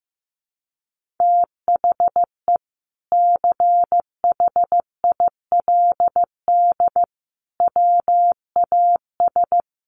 Morse code, heard as a keyed tone, about 15 wpm, 700 Hz. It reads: THE CHILD WAS
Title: Sentences from Top 1400 Words 15wpm